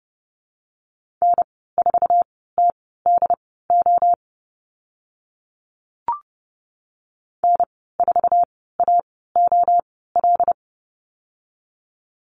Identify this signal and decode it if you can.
Morse code, heard as a keyed tone, about 30 wpm, spaced as for 17 wpm, 700 Hz. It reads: N4TDO N4AOL
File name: Lesson 14g - Number 2 - Call Signs 30wpm